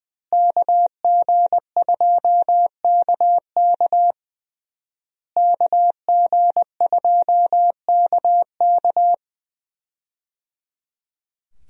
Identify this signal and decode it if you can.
Morse code, heard as a keyed tone, about 21 wpm, 700 Hz. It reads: KG2KK KG2KK
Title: USA Call Signs - 5 Characters - Repeated 3x - 20wpm